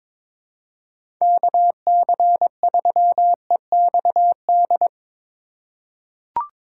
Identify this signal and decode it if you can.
Morse code, heard as a keyed tone, about 22 wpm, 700 Hz. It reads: KC3EXD
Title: USA Call Signs - 6 Characters - 4x Character Spacing - 22wpm